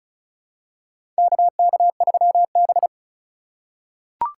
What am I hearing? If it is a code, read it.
Morse code, heard as a keyed tone, about 34 wpm, 700 Hz. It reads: KK3B
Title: USA Call Signs - 4 Characters - 3x Character Spacing - 35wpm